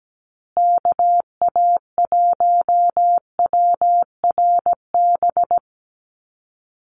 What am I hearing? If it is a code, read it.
Morse code, heard as a keyed tone, about 17 wpm, 700 Hz. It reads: KA1WRB
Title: USA Call Signs Repeated 3x 17wpm